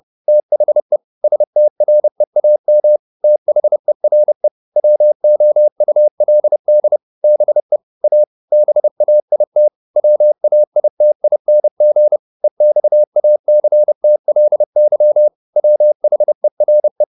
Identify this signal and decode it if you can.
Morse code, heard as a keyed tone, about 30 wpm, 600 Hz. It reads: THE STREAM THERE WOULD BE A BAIT WAITING EXACTLY WHERE
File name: the old man and the sea / the old man and the sea 1 30wpm